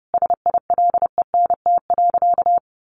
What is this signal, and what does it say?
Morse code, heard as a keyed tone, about 30 wpm, 700 Hz. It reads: SILENT.
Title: A Sound Of Thunder 30wpm